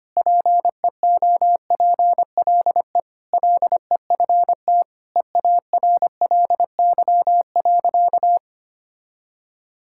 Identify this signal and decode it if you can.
Morse code, heard as a keyed tone, about 25 wpm, 700 Hz. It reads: PEOPLE LEFT EARLY.
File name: practice / 25wpm / Sentences with 3-Word Phrases - 25wpm